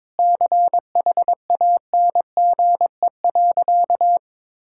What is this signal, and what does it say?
Morse code, heard as a keyed tone, about 21 wpm, 700 Hz. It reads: CHANGE.
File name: practice / 22wpm / Sentences with 3-Word Phrases - 22wpm